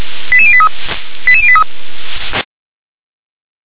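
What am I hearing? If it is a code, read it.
Five-tone selective call (ZVEI), heard as quick runs of five tones, tones of about 70 ms. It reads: 88083 88083
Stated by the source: ZVEI five-tone selective call received on an SDR.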